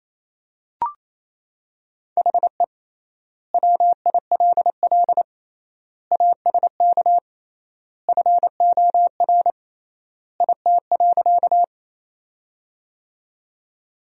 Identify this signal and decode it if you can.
Morse code, heard as a keyed tone, about 27 wpm, 700 Hz. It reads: HE WILL ASK FOR IT.
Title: 3x Word-Spacing - Sentences from Top 200 Words - 28wpm